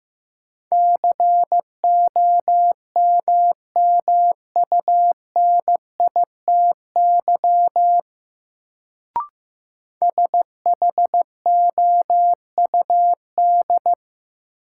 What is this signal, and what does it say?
Morse code, heard as a keyed tone, about 15 wpm, 700 Hz. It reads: COMMUNITY SHOUD
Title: Sentences from Top 1000 Words - Review Part 4 - New Word Drill - 15wpm